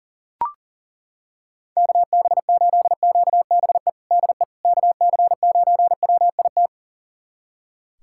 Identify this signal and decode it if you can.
Morse code, heard as a keyed tone, about 37 wpm, 700 Hz. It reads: KB8QBE DE KC9WIT
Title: USA Callsign DE Callsign 40wpm